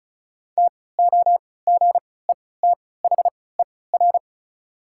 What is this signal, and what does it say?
Morse code, heard as a keyed tone, about 36 wpm, spaced as for 20 wpm, 700 Hz. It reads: TOGETHER
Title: 3x Character Spacing - Top 300 Words - 35wpm